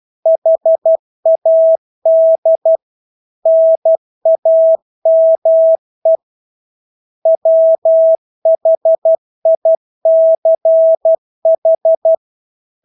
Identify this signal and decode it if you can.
Morse code, heard as a keyed tone, about 12 wpm, 650 Hz. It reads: HAD NAME WHICH
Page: Engelske ord 12 wpm